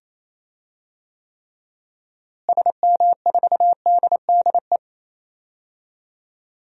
Morse code, heard as a keyed tone, about 28 wpm, 700 Hz. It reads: SM4DDE
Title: DXCC-100 Call Signs - 6 Characters - Repeated 2x - 28wpm